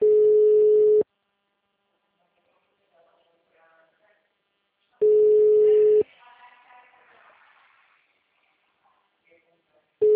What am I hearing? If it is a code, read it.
Ringback tone (Europe, 425 Hz), with speech.